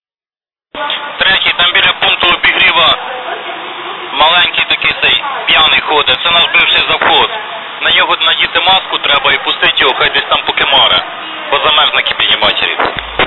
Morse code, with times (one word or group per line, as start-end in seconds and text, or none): none